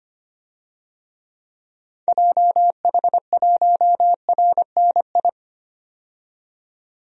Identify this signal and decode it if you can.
Morse code, heard as a keyed tone, about 25 wpm, 700 Hz. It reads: JH1RNI
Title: DXCC-100 Call Signs - 6 Characters - 25wpm